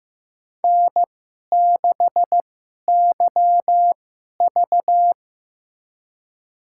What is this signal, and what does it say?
Morse code, heard as a keyed tone, about 15 wpm, spaced as for 11 wpm, 700 Hz. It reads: N6YV
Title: USA Call Signs - 4 Characters - 2x Character Spacing - 15wpm